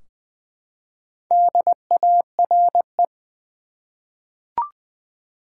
Morse code, heard as a keyed tone, about 20 wpm, 700 Hz. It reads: DARE
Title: Four Letter Words - 20wpm